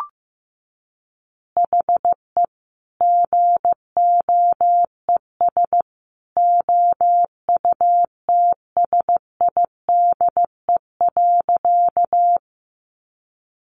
Morse code, heard as a keyed tone, about 15 wpm, 700 Hz. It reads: HE GOES OUTSIDE.
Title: Additional Easy Phrases 15wpm